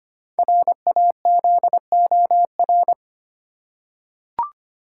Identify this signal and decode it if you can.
Morse code, heard as a keyed tone, about 25 wpm, 700 Hz. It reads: RAZOR
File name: practice / 25wpm / Easy Words with Hints - 25wpm